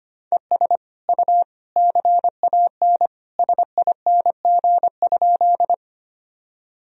Morse code, heard as a keyed tone, about 25 wpm, 700 Hz. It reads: ES U CAN SING?
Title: Sentences from Top 400 Words and Abbreviations 25wpm